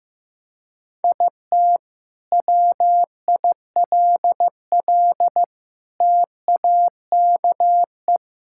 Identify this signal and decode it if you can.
Morse code, heard as a keyed tone, about 14 wpm, 700 Hz. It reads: IT WILL TAKE
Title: Sentences from Top 1000 Words - Review Part 6 - 15wpm